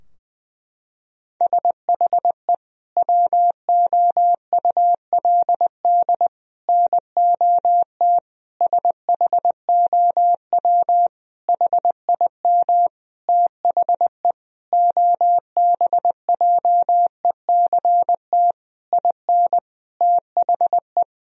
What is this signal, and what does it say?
Morse code, heard as a keyed tone, about 21 wpm, 700 Hz. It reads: SHE WOULD NOT SHOW HIM THE OBJECT IN THE
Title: Sentences from Top 1000 Words - Review Part 3 - 20wpm